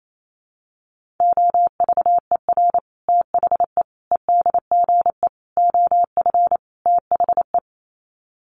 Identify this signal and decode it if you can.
Morse code, heard as a keyed tone, about 28 wpm, 700 Hz. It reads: OVER THE EDGE OF THE
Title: Sets of 5 Words Encore2 28wpm